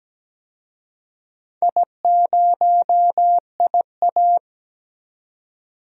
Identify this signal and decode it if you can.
Morse code, heard as a keyed tone, about 17 wpm, 700 Hz. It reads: I0IA
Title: DXCC-100 Call Signs - 4 Characters - 17wpm